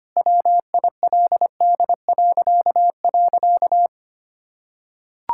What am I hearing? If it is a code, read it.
Morse code, heard as a keyed tone, about 25 wpm, 700 Hz. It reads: WILD..
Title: Pangram Challenge v2 - 25wpm